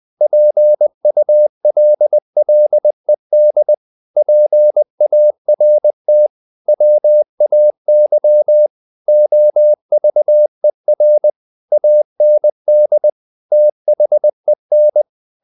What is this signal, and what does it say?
Morse code, heard as a keyed tone, about 20 wpm, 600 Hz. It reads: PULLED PART WAY OVER AND THEN
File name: the old man and the sea part 2 20wpm